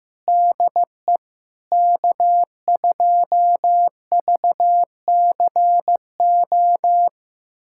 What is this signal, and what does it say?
Morse code, heard as a keyed tone, about 15 wpm, 700 Hz. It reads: DE K2VCO
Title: USA Callsign DE Callsign - 5 Characters - 15wpm